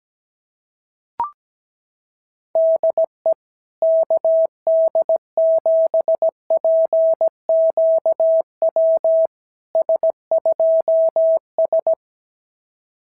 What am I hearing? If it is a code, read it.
Morse code, heard as a keyed tone, about 17 wpm, 650 Hz. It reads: DE KD7PQW S2S
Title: SOTA QSO Practice 17wpm